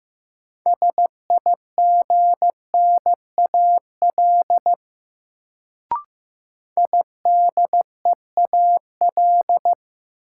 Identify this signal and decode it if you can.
Morse code, heard as a keyed tone, about 15 wpm, 700 Hz. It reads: SIGNAL IDEAL
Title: Sentences from Top 1400 Words - New Word Drill Alt - 15wpm